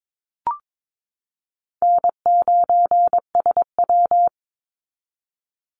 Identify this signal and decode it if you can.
Morse code, heard as a keyed tone, about 22 wpm, 700 Hz. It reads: N9SW
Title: Easy Call Signs Repeated 3x - 22wpm